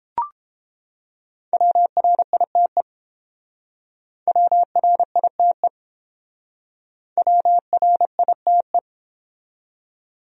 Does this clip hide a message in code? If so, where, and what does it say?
Yes: Morse code from 1.5 to 8.8 s, heard as a keyed tone, about 30 wpm, 700 Hz. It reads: WRITE WRITE WRITE